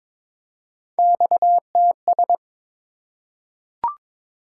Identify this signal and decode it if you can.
Morse code, heard as a keyed tone, about 22 wpm, 700 Hz. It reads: XTS
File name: practice / 22wpm / Three Letter-Number Combinations - 22wpm